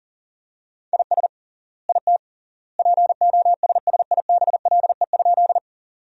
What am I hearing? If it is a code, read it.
Morse code, heard as a keyed tone, about 40 wpm, 700 Hz. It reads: IS IT POSSIBLE?
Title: 3x Word-Spacing - Sentences from Top 500 Words - 40wpm